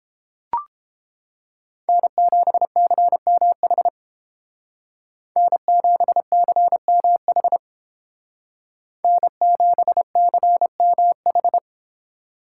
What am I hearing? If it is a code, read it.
Morse code, heard as a keyed tone, about 30 wpm, 700 Hz. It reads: N7CMH N7CMH N7CMH